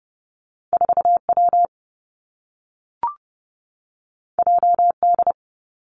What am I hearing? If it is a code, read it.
Morse code, heard as a keyed tone, about 30 wpm, 700 Hz. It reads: 4 W J D
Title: Two Challenging Letter-Number Combinations 30wpm